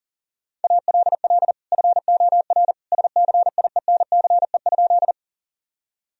Morse code, heard as a keyed tone, about 40 wpm, 700 Hz. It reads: ALL FOR SCIENCE?
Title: Sentences from Top 1000 Words - Review Part 2 - 40wpm